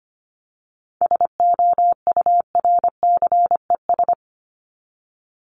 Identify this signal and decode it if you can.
Morse code, heard as a keyed tone, about 25 wpm, 700 Hz. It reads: SOURCES
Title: ES:S Suffix - Top 100 Words with Base Words 25wpm